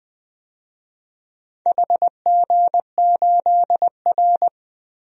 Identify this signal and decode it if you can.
Morse code, heard as a keyed tone, about 20 wpm, 700 Hz. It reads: HG8R
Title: DXCC-100 Call Signs - 4 Characters - Repeated 2x - 20wpm